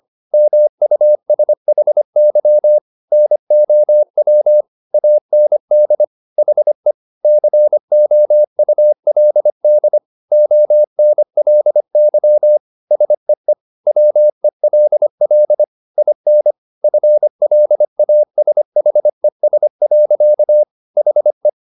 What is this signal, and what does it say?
Morse code, heard as a keyed tone, about 25 wpm, 600 Hz. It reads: MUSHY NOW AND HE COULD ONLY SEE WELL IN FLASHES. HE